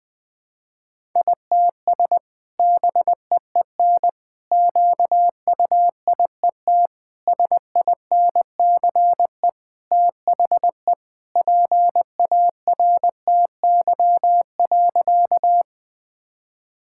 Morse code, heard as a keyed tone, about 21 wpm, 700 Hz. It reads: ITS BEEN QUIET SINCE THE PARTY.
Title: Sentences with 2-Word Phrases - 20wpm